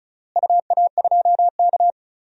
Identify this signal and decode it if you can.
Morse code, heard as a keyed tone, about 34 wpm, 700 Hz. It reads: UA2K
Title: DXCC-100 Call Signs - 4 Characters - Repeated 3x - 35wpm